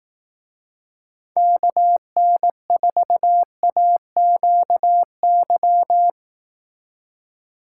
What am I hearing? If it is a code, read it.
Morse code, heard as a keyed tone, about 18 wpm, 700 Hz. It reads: KN4AQY
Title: Speed Racing - USA Call Signs 15wpm